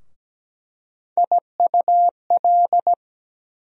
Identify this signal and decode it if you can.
Morse code, heard as a keyed tone, about 17 wpm, 700 Hz. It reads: IUL
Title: USA Call Sign Postfixes - 3 Letters 17wpm